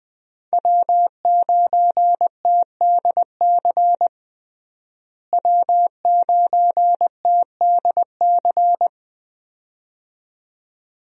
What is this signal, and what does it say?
Morse code, heard as a keyed tone, about 20 wpm, 700 Hz. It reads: W9TDC W9TDC
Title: USA Call Signs - 5 Characters - Repeated 2x - 20wpm